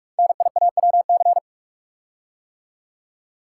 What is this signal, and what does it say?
Morse code, heard as a keyed tone, about 44 wpm, 700 Hz. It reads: NIAWC
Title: Code Groups - 5 Letters - 45wpm